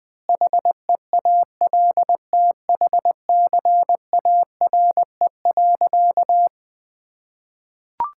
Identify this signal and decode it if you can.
Morse code, heard as a keyed tone, about 21 wpm, 700 Hz. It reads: HEALTHCARE.
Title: Sentences with 7-Word Phrases - Spoken Beginning - 20wpm